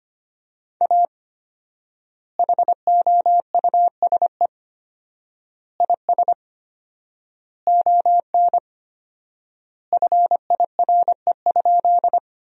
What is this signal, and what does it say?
Morse code, heard as a keyed tone, about 25 wpm, 700 Hz. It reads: A HOUSE IS ON FIRE?
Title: 4x Word-Spacing - Sentences from Top 400 Words - 25wpm